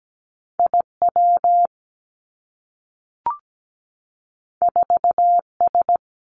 Morse code, heard as a keyed tone, about 17 wpm, 700 Hz. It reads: IW 4S
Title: Two Challenging Letter-Number Combinations 17wpm